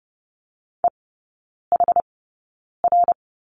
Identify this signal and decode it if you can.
Morse code, heard as a keyed tone, about 30 wpm, spaced as for 9 wpm, 700 Hz. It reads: EHR
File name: Lesson 18e - Letter m - Three Characters 30wpm